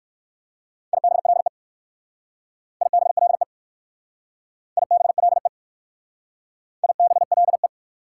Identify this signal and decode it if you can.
Morse code, heard as a keyed tone, about 55 wpm, 700 Hz. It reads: IBLE IBLE IBLE IBLE
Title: Speed-Racing Prefixes and Suffixes 45wpm